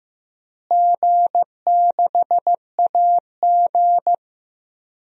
Morse code, heard as a keyed tone, about 15 wpm, 700 Hz. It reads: G6AG
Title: DXCC-100 Call Signs - 4 Characters - Repeated 3x - 15wpm